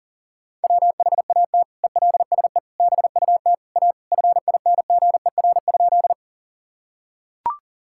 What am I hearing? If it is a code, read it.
Morse code, heard as a keyed tone, about 40 wpm, 700 Hz. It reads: WHAT ELSE BUT A FINGER?
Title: Sentences from Top 1600 Words 40wpm